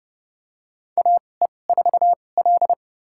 Morse code, heard as a keyed tone, about 30 wpm, spaced as for 21 wpm, 700 Hz. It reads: AE4L
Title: USA Call Signs - 2x Character Spacing - 30wpm